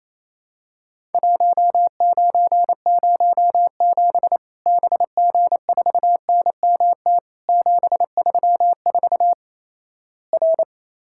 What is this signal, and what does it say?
Morse code, heard as a keyed tone, about 28 wpm, 700 Hz. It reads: 1907 BG4NMT 734 R
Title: Worked All Europe DX Contest - QTC Exchange - 28wpm